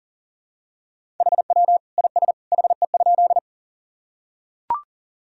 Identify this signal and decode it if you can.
Morse code, heard as a keyed tone, about 40 wpm, 700 Hz. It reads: HW IS HE?
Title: Sentences from Top 100 Words and Abbreviations 40wpm